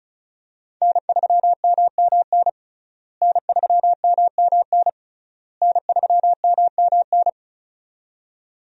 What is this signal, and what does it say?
Morse code, heard as a keyed tone, about 34 wpm, 700 Hz. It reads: N3MMN N3MMN N3MMN
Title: USA Call Signs - 5 Characters - Repeated 3x - 35wpm